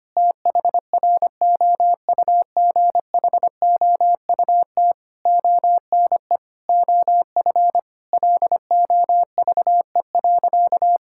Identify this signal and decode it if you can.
Morse code, heard as a keyed tone, about 25 wpm, 700 Hz. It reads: THROUGHOUT ONE OF LOVE.
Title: Sentences from Top 1000 Words - Review Part 5 - 25wpm